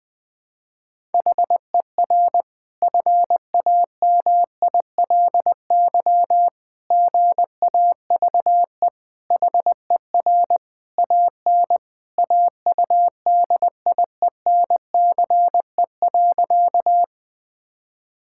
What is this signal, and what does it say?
Morse code, heard as a keyed tone, about 20 wpm, 700 Hz. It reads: HER FAMILY GAVE HER AN AUDIENCE.
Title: Sentences from Top 1000 Words - Review Part 5 - 20wpm